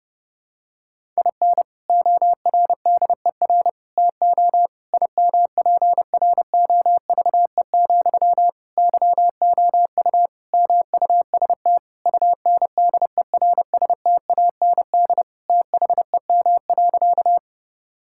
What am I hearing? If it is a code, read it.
Morse code, heard as a keyed tone, about 30 wpm, 700 Hz. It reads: IN ORDER TO IMPROVE, YOU MUST UNDERSTAND THEM.
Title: Sentences from Top 1000 Words - Review Part 5 - 30wpm